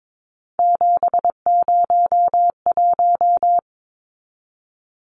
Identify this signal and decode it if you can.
Morse code, heard as a keyed tone, about 22 wpm, 700 Hz. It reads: 701
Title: Numbers - 3-Digit 22wpm